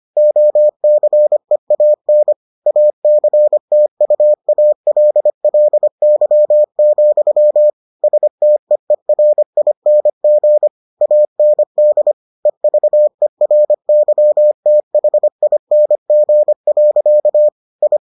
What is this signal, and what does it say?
Morse code, heard as a keyed tone, about 25 wpm, 600 Hz. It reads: OCEAN ACTUALLY, STEERING AND EVERYTHING. I